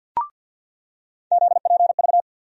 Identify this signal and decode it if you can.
Morse code, heard as a keyed tone, about 48 wpm, 700 Hz. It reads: ZPV
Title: USA Call Sign Postfixes - 3 Letters 50wpm